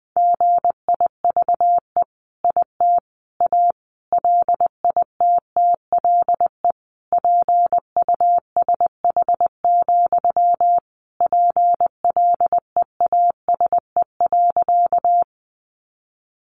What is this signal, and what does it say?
Morse code, heard as a keyed tone, about 20 wpm, 700 Hz. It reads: GIVE IT A LITTLE PUSH, PLEASE.
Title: Sentences from Top 1500 Words 20wpm